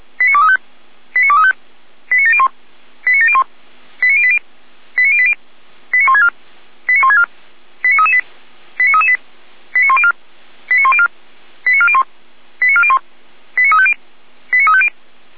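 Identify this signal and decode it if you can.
Five-tone selective call (ZVEI), heard as quick runs of five tones, tones of about 80 ms. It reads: 89235 89235 89701 89701 89080 89080 89164 89164 89338 89338 89104 89104 89401 89401 89350 89350
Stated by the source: ZVEI five-tone selective call 89590 received by SDR, recorded on the feuerwehr-orlamuende fire brigade server.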